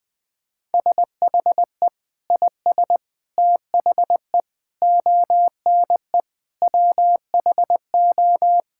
Morse code, heard as a keyed tone, about 20 wpm, 700 Hz. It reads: SHE IS THE ONE WHO
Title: Sentences with 2-Word Phrases - 20wpm